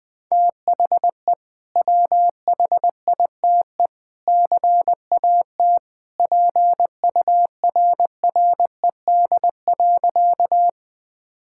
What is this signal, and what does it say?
Morse code, heard as a keyed tone, about 21 wpm, 700 Hz. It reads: THE WHITE CAT PURRED.
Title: Sentences with 2-Word Phrases - 20wpm